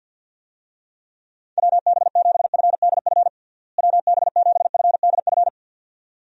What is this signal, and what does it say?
Morse code, heard as a keyed tone, about 46 wpm, 700 Hz. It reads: WB7FDF WB7FDF
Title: USA Call Signs - 6 Characters - Repeated 2x - 50wpm